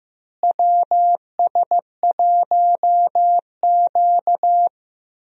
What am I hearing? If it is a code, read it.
Morse code, heard as a keyed tone, about 15 wpm, 700 Hz. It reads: WS1Q
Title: USA Call Signs - 4 Characters - Repeated 3x - 15wpm